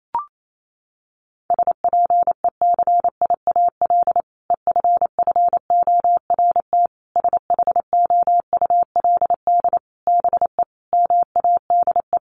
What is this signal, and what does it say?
Morse code, heard as a keyed tone, about 28 wpm, 700 Hz. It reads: SPECIAL EFFORT SHOULD BE MADE
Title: Sentences from Top 1200 Words 28wpm